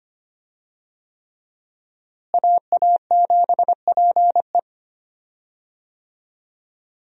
Morse code, heard as a keyed tone, about 25 wpm, 700 Hz. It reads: AA7PE
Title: USA Call Signs - 5 Characters - Repeated 3x - 25wpm